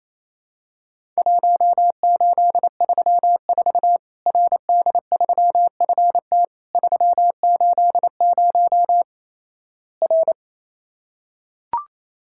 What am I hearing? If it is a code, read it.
Morse code, heard as a keyed tone, about 28 wpm, 700 Hz. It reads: 1834 RD3FT 380 R
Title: Worked All Europe DX Contest - QTC Exchange - 28wpm